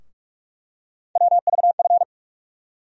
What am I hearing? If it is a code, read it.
Morse code, heard as a keyed tone, about 46 wpm, 700 Hz. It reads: WVF
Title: USA Call Sign Postfixes - 3 Letters 45wpm